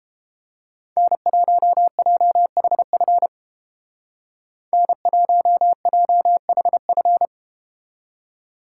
Morse code, heard as a keyed tone, about 30 wpm, 700 Hz. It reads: N1JHF N1JHF